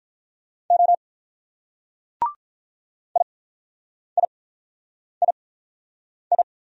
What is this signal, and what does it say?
Morse code, heard as a keyed tone, about 45 wpm, spaced as for 9 wpm, 700 Hz. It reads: O IIII